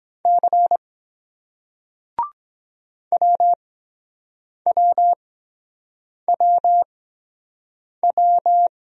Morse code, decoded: C WWWW

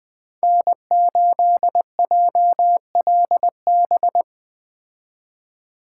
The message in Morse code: N8JLB

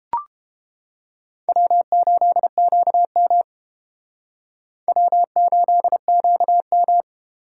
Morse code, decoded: W8QM W8QM

